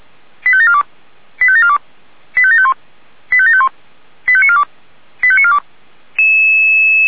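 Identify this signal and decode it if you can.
Five-tone selective call (ZVEI), heard as quick runs of five tones, tones of about 70 ms. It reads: 85742 85742 85751 85751 85942 85942
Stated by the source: ZVEI five-tone call received on an SDR.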